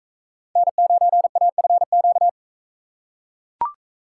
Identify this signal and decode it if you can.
Morse code, heard as a keyed tone, about 41 wpm, 700 Hz. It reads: N9AFQ